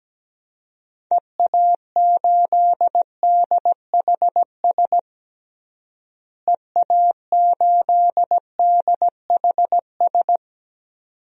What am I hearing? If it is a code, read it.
Morse code, heard as a keyed tone, about 17 wpm, 700 Hz. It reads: EA8DHS EA8DHS
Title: DXCC-100 Call Signs - 6 Characters - Repeated 2x - 17wpm